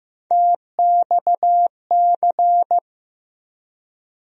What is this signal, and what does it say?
Morse code, heard as a keyed tone, about 16 wpm, 700 Hz. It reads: TXC
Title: USA Call Sign Postfixes - 3 Letters 15wpm